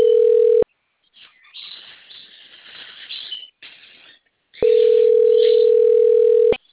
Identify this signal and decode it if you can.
Ringback tone (North America), 440 + 480 Hz.